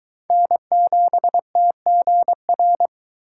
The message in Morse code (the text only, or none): N7TGR